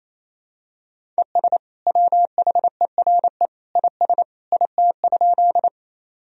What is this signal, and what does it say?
Morse code, heard as a keyed tone, about 28 wpm, 700 Hz. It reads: ES WHERE IS IT?
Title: Sentences from Top 200 Words and Abbreviations 28wpm